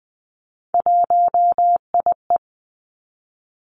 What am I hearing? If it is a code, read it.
Morse code, heard as a keyed tone, about 20 wpm, 700 Hz. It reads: 1IE
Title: Three Challenging Letter-Number Combinations 20wpm